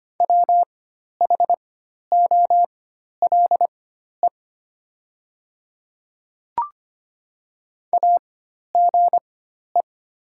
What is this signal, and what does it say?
Morse code, heard as a keyed tone, about 25 wpm, spaced as for 12 wpm, 700 Hz. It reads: WHOLE AGE
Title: 4x Character Spacing - Top 500 Words - 25wpm